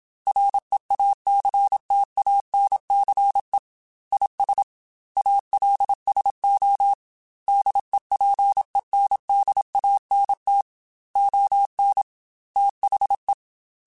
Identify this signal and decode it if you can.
Morse code, heard as a keyed tone, about 27 wpm, 800 Hz. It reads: REACTANCE IS ALSO DEPENDANT ON THE
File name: Technical Subjects / Technical Subjects-mp3s / XC @25